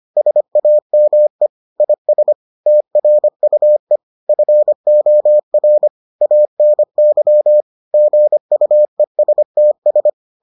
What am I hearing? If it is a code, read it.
Morse code, heard as a keyed tone, about 25 wpm, 600 Hz. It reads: SAME IS TRUE FOR ANY GUESTS